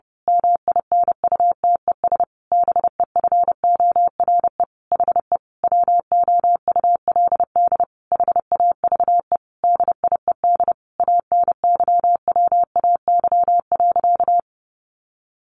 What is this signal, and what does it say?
Morse code, heard as a keyed tone, about 30 wpm, 700 Hz. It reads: MINUTES BEFORE HE WOULD HAVE DIED ANYWAY.
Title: A Sound Of Thunder 30wpm